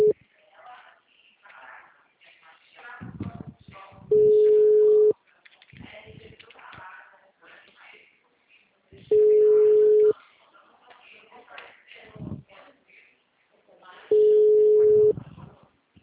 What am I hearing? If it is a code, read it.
Ringback tone (Europe, 425 Hz), with speech.
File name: out- -2032- - - .9297